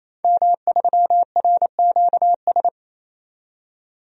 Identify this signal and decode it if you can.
Morse code, heard as a keyed tone, about 28 wpm, 700 Hz. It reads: M3RQS
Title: DXCC-100 Call Signs - 5 Characters - 28wpm